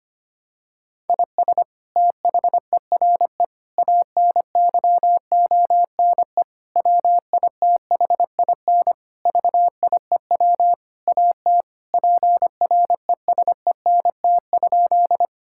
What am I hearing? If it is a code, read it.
Morse code, heard as a keyed tone, about 25 wpm, 700 Hz. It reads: IS THERE ANYONE WITHIN VIEW AT PRESENT?
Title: Sentences from Top 1000 Words - Review Part 6 - 25wpm